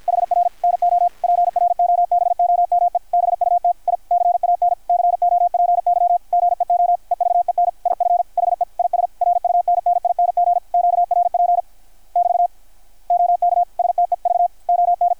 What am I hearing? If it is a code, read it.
Morse code, heard as a keyed tone, about 53 wpm, 700 Hz. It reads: HR NO PROBOME BUT I CAN COPY GEO EVEN IF HE IS RUNNING QRP <BT> OK STEV ON
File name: archived / CRSnet